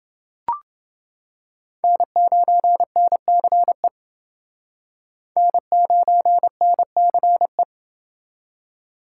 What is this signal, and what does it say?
Morse code, heard as a keyed tone, about 28 wpm, 700 Hz. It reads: N9NCE N9NCE